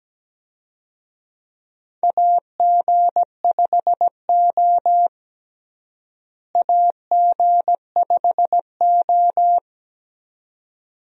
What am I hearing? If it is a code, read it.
Morse code, heard as a keyed tone, about 17 wpm, 700 Hz. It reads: AG5O AG5O
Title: USA Call Signs - 4 Characters - Repeated 2x - 17wpm